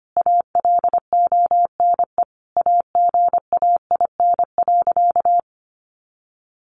Morse code, heard as a keyed tone, about 25 wpm, 700 Hz. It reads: ALONE AGAIN.
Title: Sentences from Top 1000 Words 25wpm